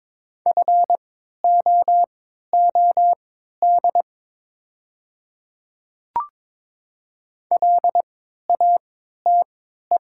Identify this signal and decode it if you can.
Morse code, heard as a keyed tone, about 22 wpm, spaced as for 12 wpm, 700 Hz. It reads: FOOD LATE
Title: 3x Character Spacing - Top 500 Words - 22wpm